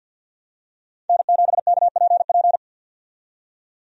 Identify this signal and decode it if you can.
Morse code, heard as a keyed tone, about 51 wpm, 700 Hz. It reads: N7KPP
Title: USA Call Signs - 5 Characters - 50wpm